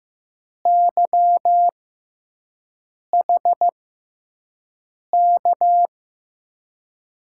Morse code, heard as a keyed tone, about 15 wpm, spaced as for 5 wpm, 700 Hz. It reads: YHK
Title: USA Call Signs - 6 Characters - 6x Character Spacing - 15wpm